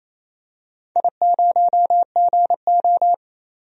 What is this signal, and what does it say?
Morse code, heard as a keyed tone, about 28 wpm, 700 Hz. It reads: I0GO
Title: DXCC-100 Call Signs - 4 Characters - 28wpm